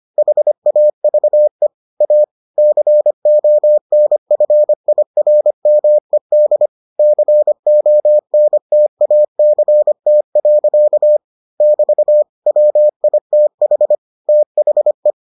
Morse code, heard as a keyed tone, about 25 wpm, 600 Hz. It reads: HAVE A CONFIRMED CONTACT. <BT> WITH THE